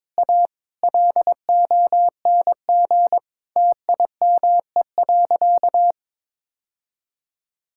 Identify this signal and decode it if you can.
Morse code, heard as a keyed tone, about 22 wpm, 700 Hz. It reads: A LONG TIME.
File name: practice / 22wpm / Sentences with 3-Word Phrases - 22wpm